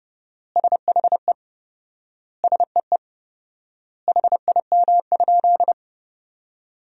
Morse code, heard as a keyed tone, about 30 wpm, 700 Hz. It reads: SHE SEE HIM?
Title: 4x Word-Spacing - Sentences from Top 100 Words - 30wpm